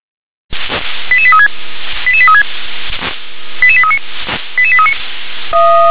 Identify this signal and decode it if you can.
Five-tone selective call (ZVEI), heard as quick runs of five tones, tones of about 70 ms. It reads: 88836 88836 88839 88839